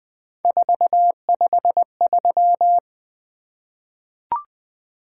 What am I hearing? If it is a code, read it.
Morse code, heard as a keyed tone, about 20 wpm, 700 Hz. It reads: 453
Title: Three Letter-Number Combinations - 20wpm